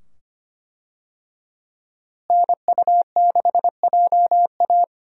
Morse code, heard as a keyed tone, about 25 wpm, 700 Hz. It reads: NU6JA
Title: USA Call Signs - 5 Characters - Repeated 2x - 25wpm